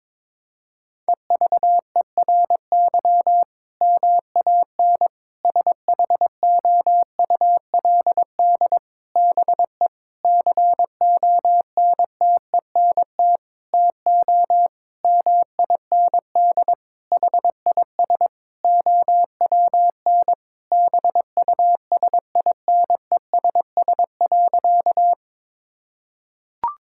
Morse code, heard as a keyed tone, about 22 wpm, 700 Hz. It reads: EVERY MAN SHOULD BE CONTENT TO MIND HIS OWN BUSINESS.
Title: Sentences from Top 1000 Words - Review Part 6 - 22wpm